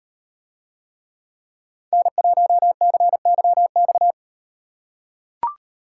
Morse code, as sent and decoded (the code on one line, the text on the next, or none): -. .---- -.-. -.-- -..-
N1CYX